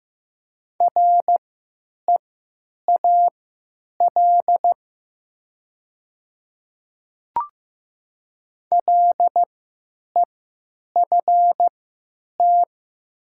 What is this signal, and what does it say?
Morse code, heard as a keyed tone, about 16 wpm, spaced as for 9 wpm, 700 Hz. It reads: REAL LEFT
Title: 3x Character Spacing - Top 300 Words - 15wpm